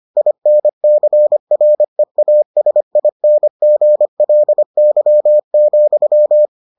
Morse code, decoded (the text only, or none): INCREASINGLY,